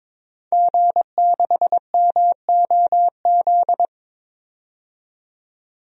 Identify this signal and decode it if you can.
Morse code, heard as a keyed tone, about 22 wpm, 700 Hz. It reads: G6MOZ
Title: DXCC-100 Call Signs - 5 Characters - 22wpm